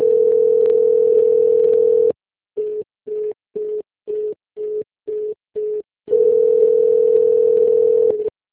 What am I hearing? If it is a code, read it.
Ringback tone (North America), 440 + 480 Hz.
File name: rg-1600-unknown- - - .24521